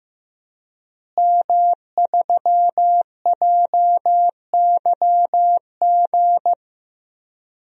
Morse code, heard as a keyed tone, about 15 wpm, 700 Hz. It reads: M3JYG
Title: DXCC-100 Call Signs - 5 Characters - Repeated 3x - 15wpm